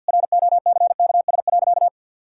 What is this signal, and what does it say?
Morse code, heard as a keyed tone, about 50 wpm, 700 Hz. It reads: ROCKS.
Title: Pangram Challenge v2 - 50wpm